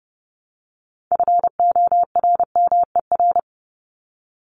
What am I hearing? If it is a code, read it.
Morse code, heard as a keyed tone, about 30 wpm, 700 Hz. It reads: FORMER
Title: LY Suffix - Top 100 Words with Base Words 30wpm